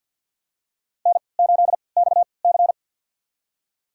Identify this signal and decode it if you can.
Morse code, heard as a keyed tone, about 52 wpm, spaced as for 29 wpm, 700 Hz. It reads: N8XC
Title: USA Call Signs - 3x Character Spacing - 50wpm